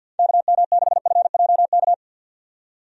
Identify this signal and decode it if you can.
Morse code, heard as a keyed tone, about 50 wpm, 700 Hz. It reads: KM6FJK
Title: USA Call Signs - 6 Characters - 50wpm